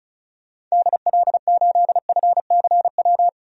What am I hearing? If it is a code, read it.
Morse code, heard as a keyed tone, about 36 wpm, 700 Hz. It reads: DL8FCW
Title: DXCC-100 Call Signs - 6 Characters - 35wpm